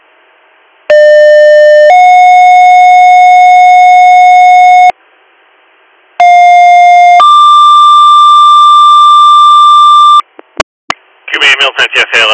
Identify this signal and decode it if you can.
Two-tone page, heard as a one-second tone, then a three-second tone, about 600 then 730 Hz, about 710 then 1180 Hz, with speech. It is heard from 0.9 to 10.2 s.